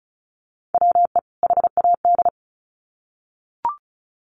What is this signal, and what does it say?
Morse code, heard as a keyed tone, about 34 wpm, 700 Hz. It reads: WE HAD
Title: Sets of 2 Words 35wpm